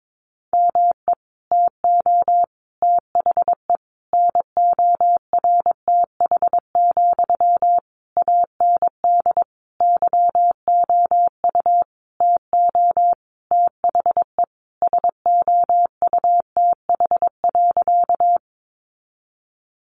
Morse code, heard as a keyed tone, about 22 wpm, 700 Hz. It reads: ME TO THE NORTH, AND YOU TO THE SOUTH.
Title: Sentences from Top 400 Words 22wpm